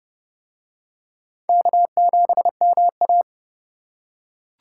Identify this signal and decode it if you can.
Morse code, heard as a keyed tone, about 27 wpm, 700 Hz. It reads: K7MA